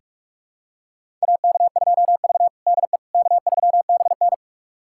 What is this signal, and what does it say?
Morse code, heard as a keyed tone, about 43 wpm, 700 Hz. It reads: AK2V DE K3BN
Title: USA Callsign DE Callsign - 4 Characters - 45wpm